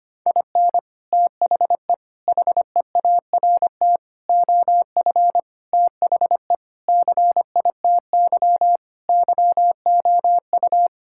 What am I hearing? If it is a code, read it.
Morse code, heard as a keyed tone, about 25 wpm, 700 Hz. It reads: IN THE HEART OF THE CITY YOU
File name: practice / 25wpm / Sentences with 3-Word Phrases - 25wpm